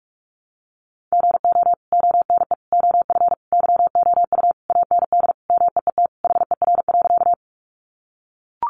Morse code, heard as a keyed tone, about 44 wpm, 700 Hz. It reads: GO ONE OF YOU AND MEET HER.
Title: Sentences from Top 1100 Words 45wpm